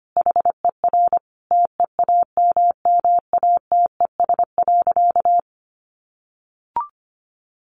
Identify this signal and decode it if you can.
Morse code, heard as a keyed tone, about 25 wpm, 700 Hz. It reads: HER TEAMMATES.
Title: Easy Phrases 25wpm